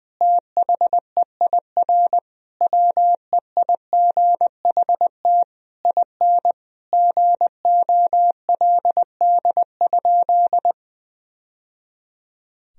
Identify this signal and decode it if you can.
Morse code, heard as a keyed tone, about 20 wpm, 700 Hz. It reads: THEIR WEIGHT IN GOLD?
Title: Sentences from Top 1000 Words - Review Part 3 - 20wpm